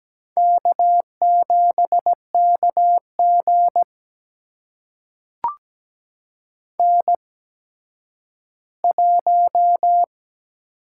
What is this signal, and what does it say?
Morse code, heard as a keyed tone, about 17 wpm, spaced as for 10 wpm, 700 Hz. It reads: K7KG N1
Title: USA Call Signs - 4 Characters - 8x Character Spacing - 17wpm